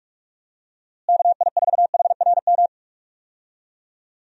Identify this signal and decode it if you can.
Morse code, heard as a keyed tone, about 41 wpm, 700 Hz. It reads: KI4HRM
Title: USA Call Signs - 6 Characters - Repeated 3x - 45wpm